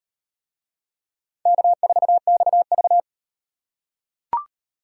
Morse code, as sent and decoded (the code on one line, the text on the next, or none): -.- ....- -..- ...-
K4XV